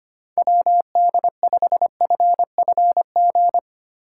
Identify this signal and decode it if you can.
Morse code, heard as a keyed tone, about 25 wpm, 700 Hz. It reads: WD5FFG
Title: USA Call Signs - 6 Characters - 6x Character Spacing - 25wpm